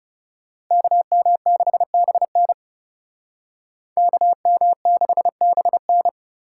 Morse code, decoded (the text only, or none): KM6BN KM6BN